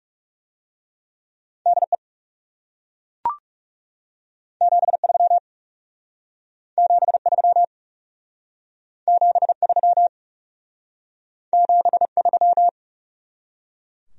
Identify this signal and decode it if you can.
Morse code, heard as a keyed tone, about 40 wpm, 700 Hz. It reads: DE 73 73 73 73